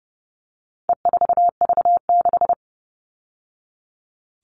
Morse code, heard as a keyed tone, about 30 wpm, 700 Hz. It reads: E4V6
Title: All Cut Numbers - 4-Digit 30wpm